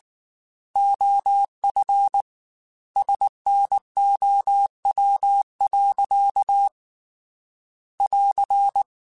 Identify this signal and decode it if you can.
Morse code, heard as a keyed tone, about 19 wpm, 800 Hz. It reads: OF SNOW. <AR>
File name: Weather Sets / Weather Sets-mp3s / Weather 3 17 @18